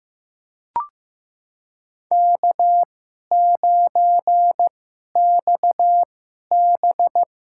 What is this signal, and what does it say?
Morse code, heard as a keyed tone, about 15 wpm, spaced as for 11 wpm, 700 Hz. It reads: K9XB
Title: USA Call Signs - 2x Character Spacing - 15wpm